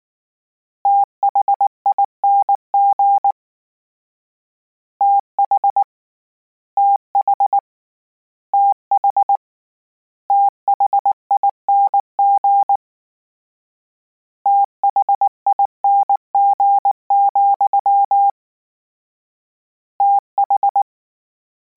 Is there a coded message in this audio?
Yes: Morse code from 0.9 to 20.8 s, heard as a keyed tone, about 19 wpm, 800 Hz. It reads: THING TH TH TH THING THING, TH